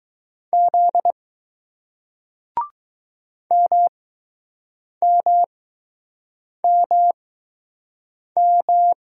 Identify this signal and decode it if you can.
Morse code, heard as a keyed tone, about 21 wpm, spaced as for 6 wpm, 700 Hz. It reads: Z MMMM